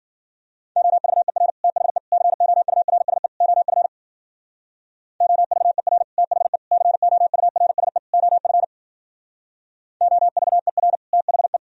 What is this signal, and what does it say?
Morse code, heard as a keyed tone, about 51 wpm, 700 Hz. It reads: OVER THE COURSE OF OVER THE COURSE OF OVER THE